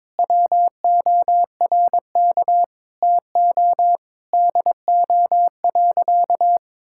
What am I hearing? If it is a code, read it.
Morse code, heard as a keyed tone, about 22 wpm, 700 Hz. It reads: WORK TO DO.
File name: Sentences with 9-Word Phrases - 22wpm